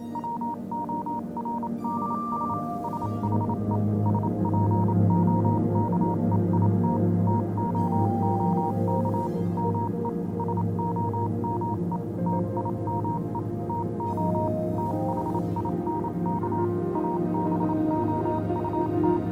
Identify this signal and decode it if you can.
Morse code, heard as a keyed tone, about 27 wpm, 940 Hz, with noise. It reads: WORDS SHE SPOKE IT TOOK ME SOME TIME TO THINK ABOUT
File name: Vylet Pony / aura / do you remember the song she sang (morse code version)